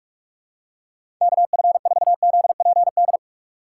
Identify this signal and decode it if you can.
Morse code, heard as a keyed tone, about 46 wpm, 700 Hz. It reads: KF4ZPD
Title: USA Call Signs - 6 Characters - 2x Character Spacing - 45wpm